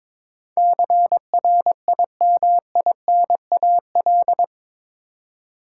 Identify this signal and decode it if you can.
Morse code, heard as a keyed tone, about 22 wpm, 700 Hz. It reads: CRIMINAL
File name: practice / 22wpm / Eight Letter Words - 22wpm